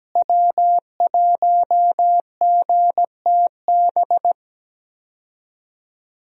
Morse code, heard as a keyed tone, about 17 wpm, 700 Hz. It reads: W1GTB
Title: USA Call Signs - 5 Characters - Repeated 2x - 17wpm